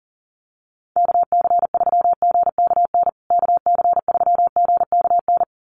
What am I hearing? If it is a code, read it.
Morse code, heard as a keyed tone, about 40 wpm, 700 Hz. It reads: KC3GKN KC3GKN
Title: USA Call Signs Repeated 2x 40wpm